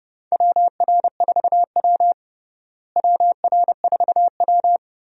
Morse code, heard as a keyed tone, about 30 wpm, 700 Hz. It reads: WR4W WR4W
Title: USA Call Signs - 4 Characters - Repeated 3x - 30wpm